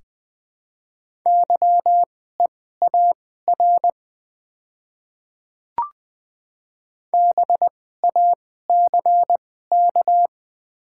Morse code, heard as a keyed tone, about 20 wpm, spaced as for 14 wpm, 700 Hz. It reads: YEAR BACK
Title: 2x Character Spacing - Top 300 Words - 20wpm